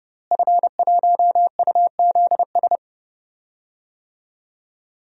Morse code, heard as a keyed tone, about 30 wpm, 700 Hz. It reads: F1UZS
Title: DXCC-100 Call Signs - 5 Characters - Repeated 3x - 30wpm